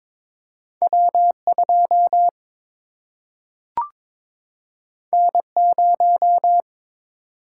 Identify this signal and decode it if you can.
Morse code, heard as a keyed tone, about 22 wpm, 700 Hz. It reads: W2 N0
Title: Single-Letter Single-Number 22wpm